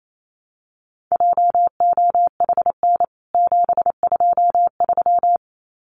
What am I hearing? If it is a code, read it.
Morse code, heard as a keyed tone, about 28 wpm, 700 Hz. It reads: JOHN 723
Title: CWT Exchange 28wpm